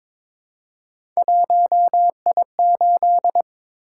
Morse code, heard as a keyed tone, about 22 wpm, 700 Hz. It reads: 1I8
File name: practice / 22wpm / Three Letter-Number Combinations - 22wpm